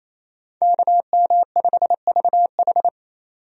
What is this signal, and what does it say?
Morse code, heard as a keyed tone, about 28 wpm, 700 Hz. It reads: KM5VH
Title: USA Call Signs - 5 Characters - 28wpm